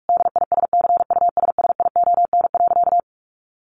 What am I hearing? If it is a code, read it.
Morse code, heard as a keyed tone, about 44 wpm, 700 Hz. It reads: DISCUSSION.
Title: Sentences from Top 1100 Words 45wpm